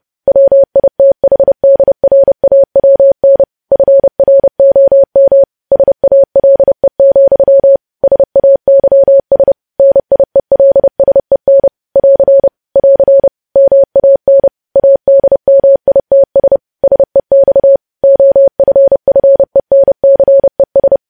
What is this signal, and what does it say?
Morse code, heard as a keyed tone, about 30 wpm, 550 Hz. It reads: WITHDRAWN FROM SALE, SAYS NIELSEN <AR> <AR> MAN ADMITS SEX OFFENCES
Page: News Headlines in Morse Code at 30 WPM